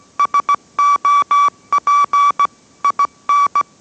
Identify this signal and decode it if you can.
Morse code, heard as a keyed tone, about 17 wpm, 1150 Hz. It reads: SOP IN